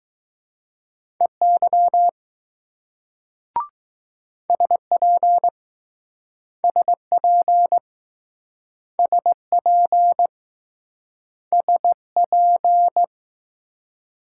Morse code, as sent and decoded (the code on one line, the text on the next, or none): . -.-- / ... .--. / ... .--. / ... .--. / ... .--.
EY SP SP SP SP